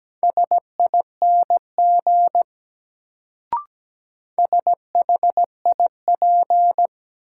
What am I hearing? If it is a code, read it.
Morse code, heard as a keyed tone, about 17 wpm, 700 Hz. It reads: SING SHIP
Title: Sentences from Top 400 Words - New Word Drill - 17wpm